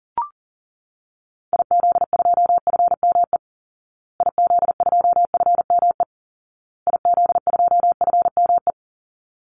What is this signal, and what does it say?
Morse code, heard as a keyed tone, about 40 wpm, 700 Hz. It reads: IZ2FME IZ2FME IZ2FME
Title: Call Signs Repeated 2x - 40wpm